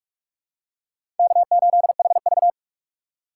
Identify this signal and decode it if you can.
Morse code, heard as a keyed tone, about 47 wpm, 700 Hz. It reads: K8HV
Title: USA Call Signs - 4 Characters - 2x Character Spacing - 45wpm